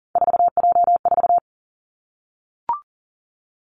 Morse code, heard as a keyed tone, about 42 wpm, 700 Hz. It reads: 4J4
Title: Three Challenging Letter-Number Combinations 40wpm